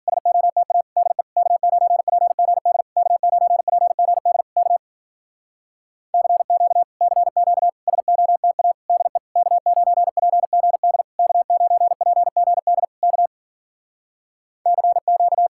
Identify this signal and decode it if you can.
Morse code, heard as a keyed tone, about 52 wpm, 700 Hz. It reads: SOTA DE K9PGD K9PGD K CQ CQ SOTA DE K9PGD K9PGD K CQ